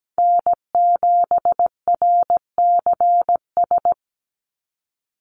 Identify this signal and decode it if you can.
Morse code, heard as a keyed tone, about 17 wpm, 700 Hz. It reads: N7RCS
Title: Call Signs Repeated 3x - 17wpm